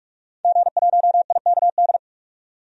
Morse code, heard as a keyed tone, about 46 wpm, 700 Hz. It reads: G1IKD
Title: DXCC-100 Call Signs - 5 Characters - 45wpm